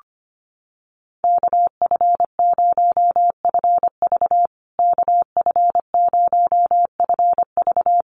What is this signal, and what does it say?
Morse code, heard as a keyed tone, about 25 wpm, 700 Hz. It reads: KF0FV KF0FV
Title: USA Call Signs Repeated 3x 25wpm